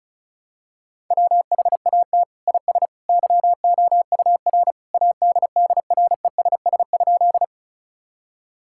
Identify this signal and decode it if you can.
Morse code, heard as a keyed tone, about 36 wpm, 700 Hz. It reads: WHAT IS YOUR ADDRESS?
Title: Sentences from Top 1000 Words - Review Part 6 - 35wpm